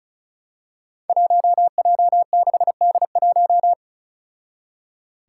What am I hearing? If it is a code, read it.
Morse code, heard as a keyed tone, about 36 wpm, 700 Hz. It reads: 1J6D1
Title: Five Challenging Letter-Number Combinations 35wpm